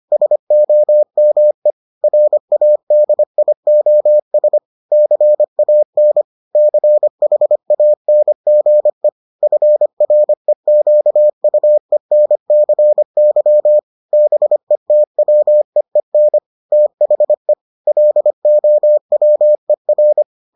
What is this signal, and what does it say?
Morse code, heard as a keyed tone, about 25 wpm, 600 Hz. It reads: SOME RADIOS CAN CHANGE FREQUENCY BETWEEN THE LOWER